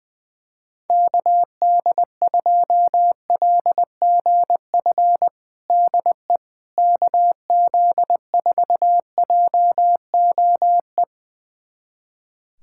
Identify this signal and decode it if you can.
Morse code, heard as a keyed tone, about 20 wpm, 700 Hz. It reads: KD2LGF DE KZ4JOE
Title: USA Callsign DE Callsign - 6 Characters - 20wpm